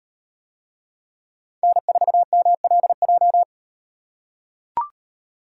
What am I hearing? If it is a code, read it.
Morse code, heard as a keyed tone, about 36 wpm, 700 Hz. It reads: N4MLJ